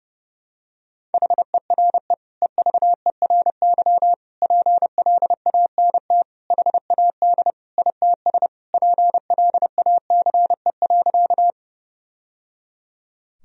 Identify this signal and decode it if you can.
Morse code, heard as a keyed tone, about 32 wpm, 700 Hz. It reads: HERE EVERY PLANT HAD ITS PLACE.
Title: Sentences from Top 1000 Words - Review Part 1 - 30wpm